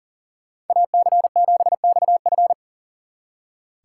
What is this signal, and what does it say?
Morse code, heard as a keyed tone, about 52 wpm, 700 Hz. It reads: AC7XF